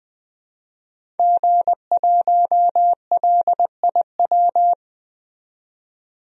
Morse code, heard as a keyed tone, about 21 wpm, 700 Hz. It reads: G1LIW
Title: DXCC-100 Call Signs - 5 Characters - 20wpm